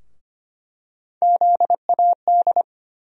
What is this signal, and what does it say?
Morse code, heard as a keyed tone, about 25 wpm, 700 Hz. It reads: ZAD
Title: USA Call Sign Postfixes - 3 Letters 25wpm